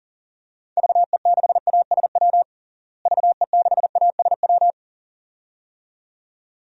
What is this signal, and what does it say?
Morse code, heard as a keyed tone, about 40 wpm, 700 Hz. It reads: VE6ASW VE6ASW
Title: DXCC-100 Call Signs - 6 Characters - Repeated 2x - 40wpm